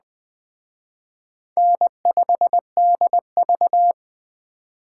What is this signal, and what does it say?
Morse code, heard as a keyed tone, about 21 wpm, 700 Hz. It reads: N5DV
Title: Speed Racing - USA 1x2 Call Signs 17wpm